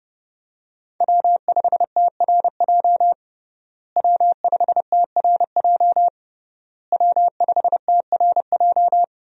Morse code, heard as a keyed tone, about 31 wpm, 700 Hz. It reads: W5TRJ W5TRJ W5TRJ
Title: USA Call Signs - 5 Characters - Repeated 3x - 30wpm